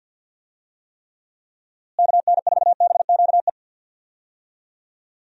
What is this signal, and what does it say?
Morse code, heard as a keyed tone, about 52 wpm, 700 Hz. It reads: KN4BQE
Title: USA Call Signs - 6 Characters - Repeated 2x - 50wpm